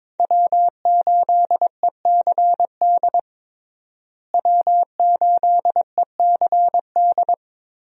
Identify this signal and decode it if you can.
Morse code, heard as a keyed tone, about 22 wpm, 700 Hz. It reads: W8ECD W8ECD
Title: USA Call Signs - 5 Characters - Repeated 3x - 22wpm